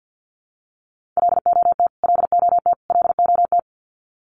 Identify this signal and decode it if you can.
Morse code, heard as a keyed tone, about 50 wpm, 700 Hz. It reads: LOT LOT LOT
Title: Top 500 Words Repeated 50wpm